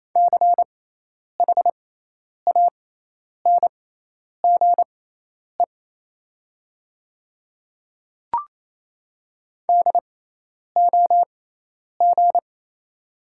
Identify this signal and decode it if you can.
Morse code, heard as a keyed tone, about 28 wpm, spaced as for 10 wpm, 700 Hz. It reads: CHANGE DOG
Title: 6x Character Spacing - Top 500 Words - 28wpm